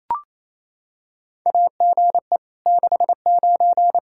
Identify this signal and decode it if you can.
Morse code, heard as a keyed tone, about 28 wpm, 700 Hz. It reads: AGE 69
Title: Practice QSO - Exchange 3 - More Info - 28wpm